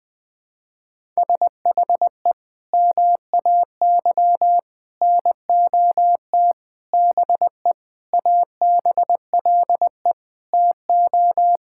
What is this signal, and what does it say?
Morse code, heard as a keyed tone, about 21 wpm, 700 Hz. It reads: SHE MAY NOT BE ABLE TO
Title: Sentences with 5-Word Phrases - Spoken Beginning - 20wpm